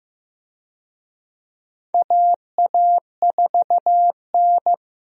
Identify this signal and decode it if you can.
Morse code, heard as a keyed tone, about 15 wpm, 700 Hz. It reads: AA4N
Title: USA Call Signs - 4 Characters - Repeated 3x - 15wpm